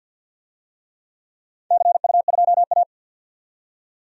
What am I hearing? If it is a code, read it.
Morse code, heard as a keyed tone, about 52 wpm, 700 Hz. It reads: CU2A
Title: DXCC-100 Call Signs - 4 Characters - 50wpm